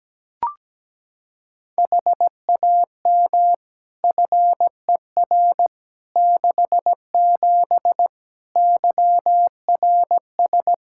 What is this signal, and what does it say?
Morse code, heard as a keyed tone, about 17 wpm, 700 Hz. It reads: HAM FER 67 YRS
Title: Practice QSO - Exchange 3 - More Info - 17wpm